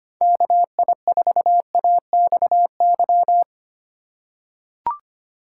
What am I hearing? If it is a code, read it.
Morse code, heard as a keyed tone, about 25 wpm, 700 Hz. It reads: KI4AXY
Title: USA Call Signs - 6 Characters - 2x Character Spacing - 25wpm